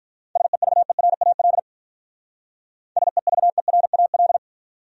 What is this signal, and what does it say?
Morse code, heard as a keyed tone, about 43 wpm, 700 Hz. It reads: SEVERAL SEVERAL
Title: Speed-Racing Top 500 Words 35wpm